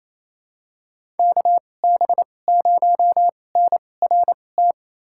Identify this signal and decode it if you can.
Morse code, heard as a keyed tone, about 28 wpm, spaced as for 20 wpm, 700 Hz. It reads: KB0NRT
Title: USA Call Signs - 6 Characters - 2x Character Spacing - 28wpm